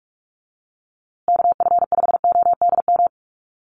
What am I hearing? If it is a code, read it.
Morse code, heard as a keyed tone, about 44 wpm, 700 Hz. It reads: KF5ODM
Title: USA Call Signs Repeated 3x 45wpm